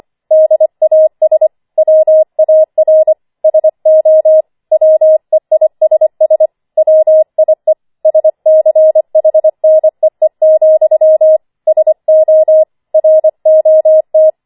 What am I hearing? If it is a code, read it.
Morse code, heard as a keyed tone, about 25 wpm, 630 Hz. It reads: DAS WAR SO WEISS WIE SCHNEE, SO ROT